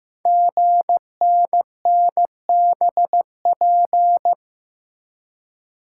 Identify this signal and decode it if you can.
Morse code, heard as a keyed tone, about 15 wpm, 700 Hz. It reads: GNNBP
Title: Code Groups - 5 Letters - 15wpm